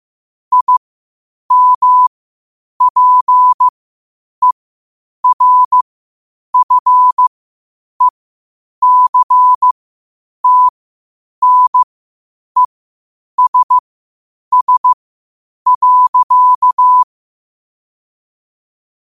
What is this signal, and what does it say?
Morse code, heard as a keyed tone, about 15 wpm, spaced as for 8 wpm, 1000 Hz. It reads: IMPERFECTNESS.
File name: qotdfiles / 8wpm